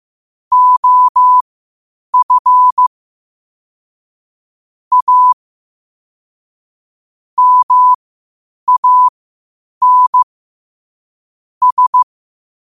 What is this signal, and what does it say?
Morse code, heard as a keyed tone, about 15 wpm, spaced as for 8 wpm, 1000 Hz. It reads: OF A MAN S
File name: qotdfiles / 8wpm